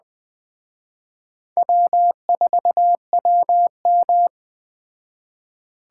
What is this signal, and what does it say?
Morse code, heard as a keyed tone, about 20 wpm, 700 Hz. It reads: W4WM
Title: Speed Racing - USA 1x2 Call Signs 17wpm